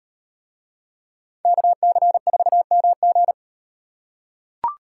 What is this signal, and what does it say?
Morse code, heard as a keyed tone, about 33 wpm, 700 Hz. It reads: KC4MG